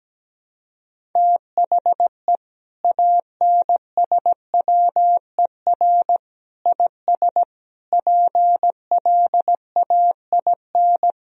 Morse code, heard as a keyed tone, about 17 wpm, 700 Hz. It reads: THE ANSWER IS PLAIN
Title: Sentences from Top 1000 Words - Review Part 2 - 17wpm